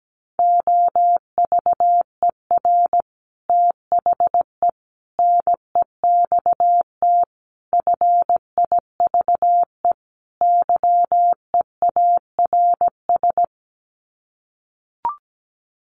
Morse code, heard as a keyed tone, about 17 wpm, 700 Hz. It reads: OVER THE NEXT FIVE YEARS
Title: Sets of 5 Words 17wpm